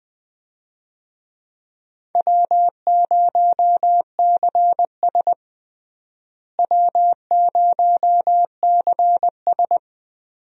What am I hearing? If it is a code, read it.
Morse code, heard as a keyed tone, about 20 wpm, 700 Hz. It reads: W0CS W0CS
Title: USA Call Signs - 4 Characters - Repeated 2x - 20wpm